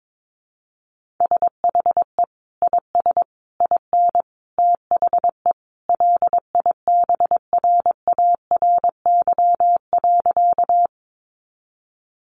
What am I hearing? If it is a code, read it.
Morse code, heard as a keyed tone, about 22 wpm, 700 Hz. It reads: SHE IS IN THE LIBRARY.
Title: Sentences from Top 600 Words 22wpm